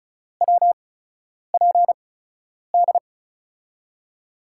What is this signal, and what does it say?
Morse code, heard as a keyed tone, about 36 wpm, spaced as for 10 wpm, 700 Hz. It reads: WPD
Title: USA Call Signs - 6 Characters - 8x Character Spacing - 35wpm